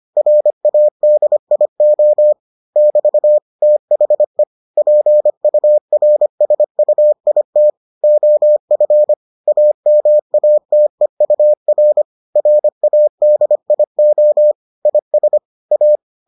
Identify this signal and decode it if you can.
Morse code, heard as a keyed tone, about 25 wpm, 600 Hz. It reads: RADIO <BT> THE PURSUIT OF AMATEUR RADIO IS A